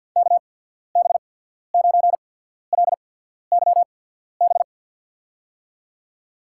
Morse code, heard as a keyed tone, about 53 wpm, spaced as for 14 wpm, 700 Hz. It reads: KB9LYB
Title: USA Call Signs - 6 Characters - 8x Character Spacing - 50wpm